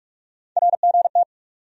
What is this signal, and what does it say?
Morse code, heard as a keyed tone, about 42 wpm, 700 Hz. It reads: RGT
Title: USA Call Sign Postfixes - 3 Letters 45wpm